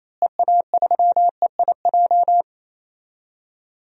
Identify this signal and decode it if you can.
Morse code, heard as a keyed tone, about 27 wpm, 700 Hz. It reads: EA3EIJ
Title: DXCC-100 Call Signs - 6 Characters - Repeated 3x - 28wpm